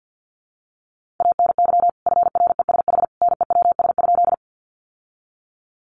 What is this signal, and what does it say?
Morse code, heard as a keyed tone, about 50 wpm, 700 Hz. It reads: ANY FRESH NEWS?
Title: Sentences from Top 1200 Words 50wpm